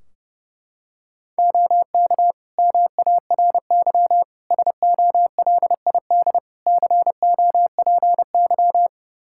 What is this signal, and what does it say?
Morse code, heard as a keyed tone, about 32 wpm, 700 Hz. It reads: OK MARY SOLID COPY
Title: Practice QSO - Exchange 3 - More Info - 30wpm